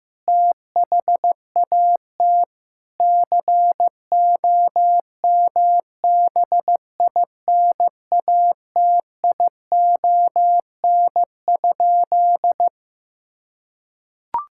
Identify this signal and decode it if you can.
Morse code, heard as a keyed tone, about 15 wpm, 700 Hz. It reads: THAT COMBINATION?
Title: Sentences from Top 1000 Words - Review Part 6 - 15wpm